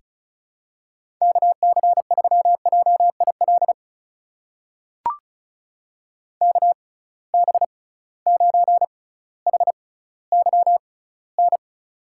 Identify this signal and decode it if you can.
Morse code, heard as a keyed tone, about 35 wpm, spaced as for 18 wpm, 700 Hz. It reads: KC3JIL KB9HYN
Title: USA Call Signs - 6 Characters - 6x Character Spacing - 35wpm